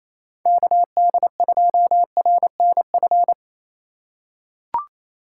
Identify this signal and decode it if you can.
Morse code, heard as a keyed tone, about 28 wpm, 700 Hz. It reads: KD2RNF
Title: USA Call Signs - 6 Characters - 6x Character Spacing - 28wpm